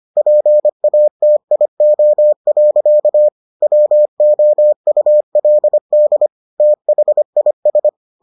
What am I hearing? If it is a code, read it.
Morse code, heard as a keyed tone, about 25 wpm, 600 Hz. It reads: PATIO. WOULD THIS